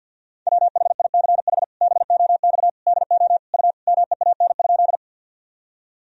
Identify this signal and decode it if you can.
Morse code, heard as a keyed tone, about 52 wpm, 700 Hz. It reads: WHICH BOX DO U MEAN?
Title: Sentences from Top 500 Words and Abbreviations 50wpm